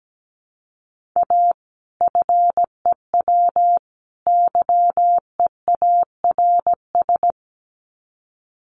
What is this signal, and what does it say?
Morse code, heard as a keyed tone, about 17 wpm, 700 Hz. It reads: A FEW YEARS
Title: Sets of 3 Words 17wpm